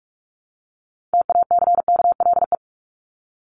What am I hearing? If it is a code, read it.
Morse code, heard as a keyed tone, about 44 wpm, 700 Hz. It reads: TACKLE
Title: Top 1751-2000 Words 45wpm